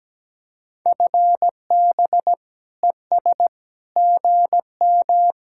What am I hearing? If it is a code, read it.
Morse code, heard as a keyed tone, about 17 wpm, 700 Hz. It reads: FB ES GM
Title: Speed-Racing Common QSO Phrases - 17wpm